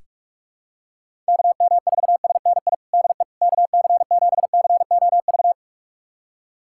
Morse code, heard as a keyed tone, about 46 wpm, 700 Hz. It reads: KM4SNI DE KC7COV
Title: USA Callsign DE Callsign 45wpm